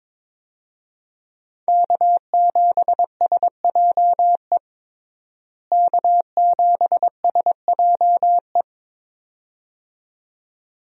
Morse code, heard as a keyed tone, about 22 wpm, 700 Hz. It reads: K7SJE K7SJE
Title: USA Call Signs - 5 Characters - Repeated 2x - 22wpm